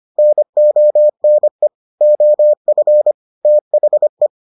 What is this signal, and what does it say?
Morse code, heard as a keyed tone, about 25 wpm, 600 Hz. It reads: NONE OF THE